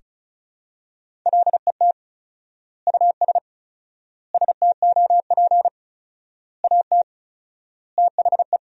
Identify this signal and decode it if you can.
Morse code, heard as a keyed tone, about 36 wpm, 700 Hz. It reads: LET US STOP AT THE
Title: 4x Word-Spacing - Sentences from Top 400 Words - 35wpm